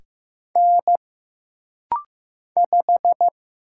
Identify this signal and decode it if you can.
Morse code, heard as a keyed tone, about 15 wpm, 700 Hz. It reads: N 5
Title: Single Letter-Number Alt - Rapid-Fire 15wpm